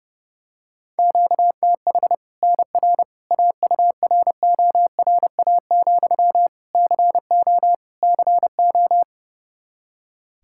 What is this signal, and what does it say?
Morse code, heard as a keyed tone, about 32 wpm, 700 Hz. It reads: QTH NR AURORA, CO CO
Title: Practice QSO - Exchange 1 - Beginning - 30wpm